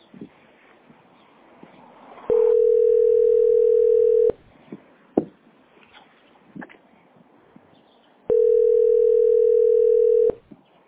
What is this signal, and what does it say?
Ringback tone (North America), 440 + 480 Hz.